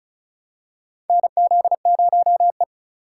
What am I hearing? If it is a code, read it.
Morse code, heard as a keyed tone, about 34 wpm, 700 Hz. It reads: NZ0E
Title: USA Call Signs - 4 Characters - 4x Character Spacing - 35wpm